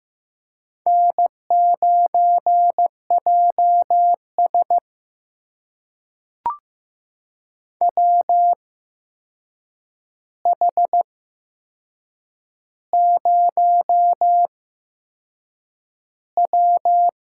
Morse code, heard as a keyed tone, about 15 wpm, spaced as for 6 wpm, 700 Hz. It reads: N9JS WH0W
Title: USA Call Signs - 4 Characters - 8x Character Spacing - 15wpm